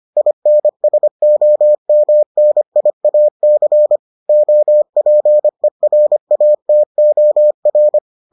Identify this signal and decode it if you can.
Morse code, heard as a keyed tone, about 25 wpm, 600 Hz. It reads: INSOMNIAC OPERATOR